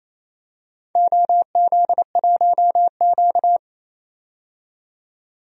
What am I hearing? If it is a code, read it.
Morse code, heard as a keyed tone, about 28 wpm, 700 Hz. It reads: OZ1Q
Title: DXCC-100 Call Signs - 4 Characters - 28wpm